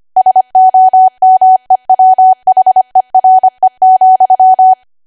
Morse code, heard as a keyed tone, about 25 wpm, 750 Hz. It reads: SOMEWHERE,
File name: cwa / W1AW-Level3 / 25WPM